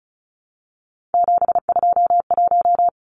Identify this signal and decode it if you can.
Morse code, heard as a keyed tone, about 34 wpm, 700 Hz. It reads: 721
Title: Numbers - 3-Digit 35wpm